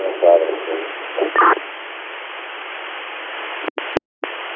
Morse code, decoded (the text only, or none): none